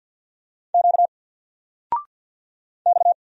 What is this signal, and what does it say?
Morse code, heard as a keyed tone, about 50 wpm, 700 Hz. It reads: Q X
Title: Single Letter-Number 50wpm